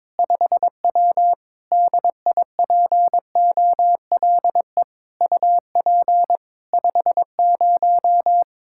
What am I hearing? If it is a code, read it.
Morse code, heard as a keyed tone, about 22 wpm, 700 Hz. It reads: 5W DIPOLE UP 50
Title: Practice QSO - Exchange 2 - Info - 22wpm